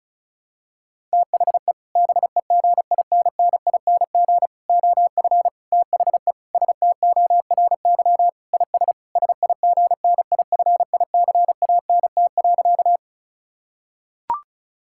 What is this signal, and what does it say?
Morse code, heard as a keyed tone, about 36 wpm, 700 Hz. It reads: THE BEGINNING OF THE STORY IS SIGNIFICANT.
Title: Sentences from Top 1000 Words - Review Part 6 - 35wpm